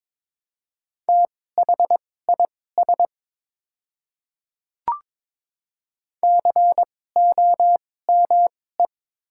Morse code, heard as a keyed tone, about 22 wpm, spaced as for 16 wpm, 700 Hz. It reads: THIS COME
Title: 2x Character Spacing - Top 100 Words - 22wpm